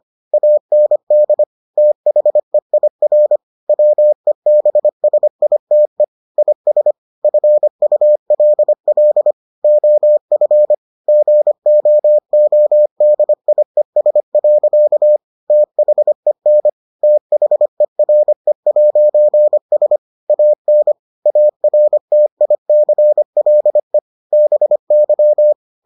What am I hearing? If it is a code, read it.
Morse code, heard as a keyed tone, about 25 wpm, 600 Hz. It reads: AND THEIR WEBSITE IS FULL OF GOODIES. THEN THERE'S AN ARTICLE BY